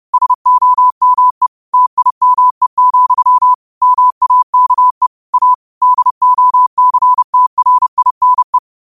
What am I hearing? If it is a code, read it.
Morse code, heard as a keyed tone, about 30 wpm, 1000 Hz. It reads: SOME TIME, MAKE A DOCTRINE